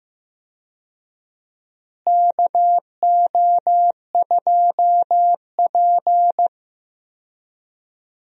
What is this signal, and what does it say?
Morse code, heard as a keyed tone, about 15 wpm, 700 Hz. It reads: KO2P
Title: USA Call Signs - 4 Characters - Repeated 2x - 15wpm